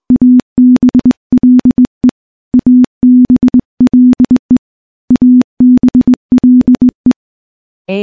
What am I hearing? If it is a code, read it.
Morse code, heard as a keyed tone, about 21 wpm, 250 Hz. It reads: ABLE ABLE ABLE
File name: morse / morse-training-20WPM